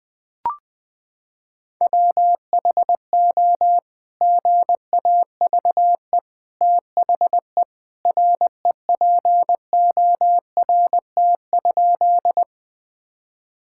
Morse code, heard as a keyed tone, about 20 wpm, 700 Hz. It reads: WHO GAVE THE REPORT?
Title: Sentences from Top 1000 Words - Review Part 6 - 20wpm